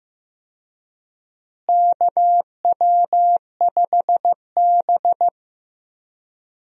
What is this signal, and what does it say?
Morse code, heard as a keyed tone, about 15 wpm, 700 Hz. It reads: KW5B
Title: USA Call Signs - 4 Characters - Repeated 3x - 15wpm